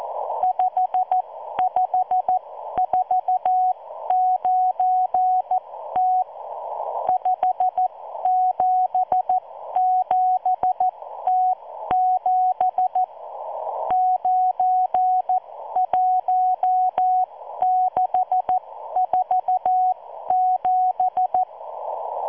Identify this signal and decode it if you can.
Morse code, heard as a keyed tone, about 14 wpm, spaced as for 12 wpm, 740 Hz. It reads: 5549T 577T7 91647